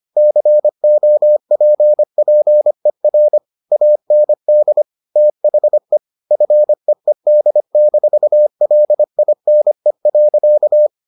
Morse code, heard as a keyed tone, about 25 wpm, 600 Hz. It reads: COPPER AND THE FEED-LINE.